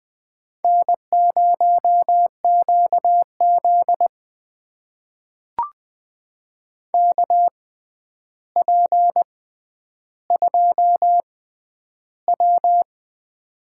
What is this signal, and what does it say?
Morse code, heard as a keyed tone, about 20 wpm, spaced as for 10 wpm, 700 Hz. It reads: N0QZ KP2W
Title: USA Call Signs - 4 Characters - 6x Character Spacing - 20wpm